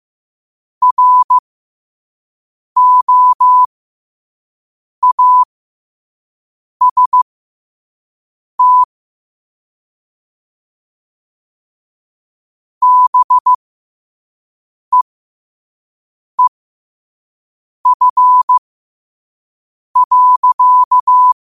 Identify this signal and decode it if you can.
Morse code, heard as a keyed tone, about 15 wpm, spaced as for 5 wpm, 1000 Hz. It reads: ROAST BEEF.
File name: qotdfiles / 5wpm